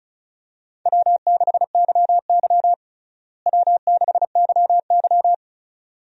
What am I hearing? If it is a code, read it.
Morse code, heard as a keyed tone, about 34 wpm, 700 Hz. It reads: W6YY W6YY
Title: USA Call Signs - 4 Characters - Repeated 2x - 35wpm